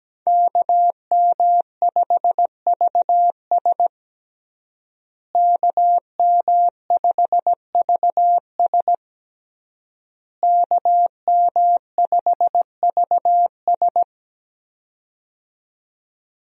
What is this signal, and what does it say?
Morse code, heard as a keyed tone, about 17 wpm, 700 Hz. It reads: KM5VS KM5VS KM5VS
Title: USA Call Signs - 5 Characters - Repeated 3x - 17wpm